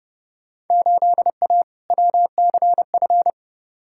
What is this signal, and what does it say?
Morse code, heard as a keyed tone, about 30 wpm, 700 Hz. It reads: 8A WCF
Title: Field Day Exchanges Encore - 30wpm